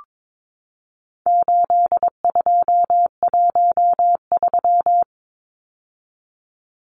Morse code, heard as a keyed tone, about 22 wpm, 700 Hz. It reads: 8213
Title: Numbers - 4-Digit 22wpm